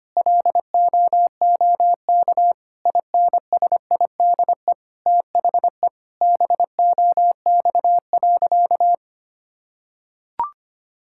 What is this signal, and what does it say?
Morse code, heard as a keyed tone, about 25 wpm, 700 Hz. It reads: LOOK INSIDE THE BOX.
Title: Sentences from Top 1000 Words - Review Part 6 - 25wpm